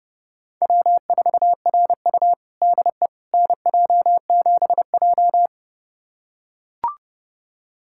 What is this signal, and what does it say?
Morse code, heard as a keyed tone, about 30 wpm, 700 Hz. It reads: W4RU DE NJ7J
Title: USA Callsign DE Callsign - 4 Characters - 30wpm